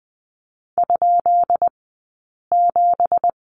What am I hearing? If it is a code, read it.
Morse code, heard as a keyed tone, about 20 wpm, 700 Hz. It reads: ? 7
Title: 165 Lesson 27d - Number 7 - Two Characters 20wpm